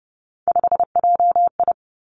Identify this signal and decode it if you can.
Morse code, heard as a keyed tone, about 30 wpm, 700 Hz. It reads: 5JI
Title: Three Challenging Letter-Number Combinations 30wpm